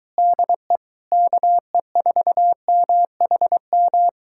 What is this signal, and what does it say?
Morse code, heard as a keyed tone, about 22 wpm, 700 Hz. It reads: DE KE4MHM